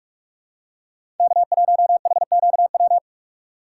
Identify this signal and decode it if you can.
Morse code, heard as a keyed tone, about 46 wpm, 700 Hz. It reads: K1HQW
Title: USA Call Signs - 5 Characters - 3x Character Spacing - 45wpm